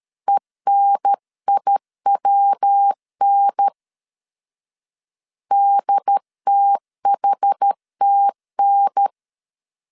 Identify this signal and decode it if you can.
Morse code, heard as a keyed tone, about 13 wpm, 800 Hz. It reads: ENIWN DTHTN
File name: CW / DET / 1 3 1500C S C 12wpm 2750